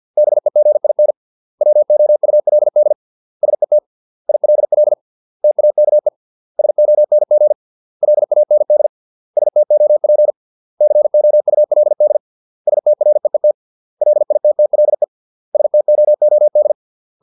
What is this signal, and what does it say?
Morse code, heard as a keyed tone, about 50 wpm, 600 Hz. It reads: BEGIN WOULD SET ILL TAKE SONG LAND STOP COULD STREET LITTLE STOOD